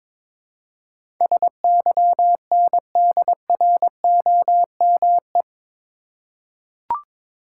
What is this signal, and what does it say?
Morse code, heard as a keyed tone, about 21 wpm, 700 Hz. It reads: SYNDROME
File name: practice / 22wpm / Eight Letter Words - 22wpm